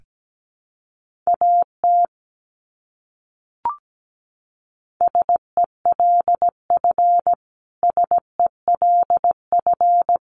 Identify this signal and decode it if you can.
Morse code, heard as a keyed tone, about 17 wpm, 700 Hz. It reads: AT SELF SELF
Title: Top 200 Words Repeated 17wpm